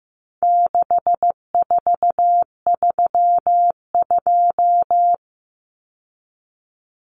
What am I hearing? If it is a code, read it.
Morse code, heard as a keyed tone, about 15 wpm, 700 Hz. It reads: 6432
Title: Common Cut Numbers - 4-Digit 15wpm